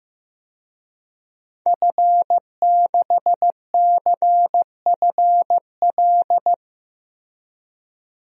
Morse code, heard as a keyed tone, about 15 wpm, 700 Hz. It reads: F6CFL
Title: DXCC-100 Call Signs - 5 Characters - Repeated 2x - 15wpm